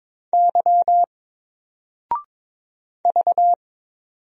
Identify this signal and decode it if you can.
Morse code, heard as a keyed tone, about 22 wpm, 700 Hz. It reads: Y V
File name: Single-Letter-Number-22wpm